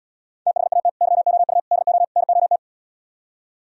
Morse code, heard as a keyed tone, about 77 wpm, 700 Hz. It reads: THAT YOU DO NOT